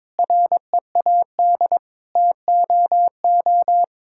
Morse code, heard as a keyed tone, about 21 wpm, 700 Hz. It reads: READ TOO
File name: Sentences with 9-Word Phrases - 22wpm